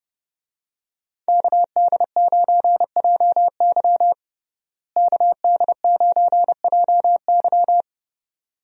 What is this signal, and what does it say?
Morse code, heard as a keyed tone, about 29 wpm, 700 Hz. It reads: KD9JY KD9JY
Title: USA Call Signs - 5 Characters - Repeated 2x - 30wpm